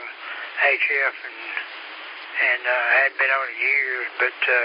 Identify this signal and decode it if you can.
Emergency Alert System (SAME) tones, heard as harsh buzzing bursts, with speech. It is heard from 0.8 to 4.3 s.